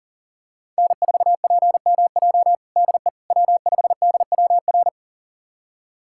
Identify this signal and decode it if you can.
Morse code, heard as a keyed tone, about 40 wpm, 700 Hz. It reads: N4PMJ DE W5DWR
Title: USA Callsign DE Callsign - 5 Characters - 40wpm